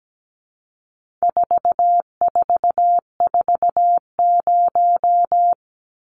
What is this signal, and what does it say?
Morse code, heard as a keyed tone, about 17 wpm, 700 Hz. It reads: 4440
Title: Numbers - 4-Digit 17wpm